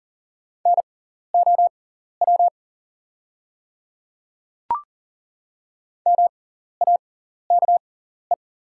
Morse code, heard as a keyed tone, about 40 wpm, spaced as for 14 wpm, 690 Hz. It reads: NOW MAKE
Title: 6x Character Spacing - Top 200 Words - 40wpm